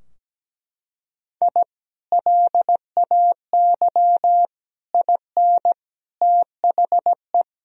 Morse code, heard as a keyed tone, about 17 wpm, 700 Hz. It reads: I LAY IN THE
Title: Sentences from Top 1000 Words - Review Part 2 - 17wpm